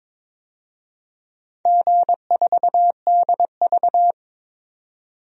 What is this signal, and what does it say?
Morse code, heard as a keyed tone, about 22 wpm, 700 Hz. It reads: G4DV
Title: DXCC-100 Call Signs - 4 Characters - 22wpm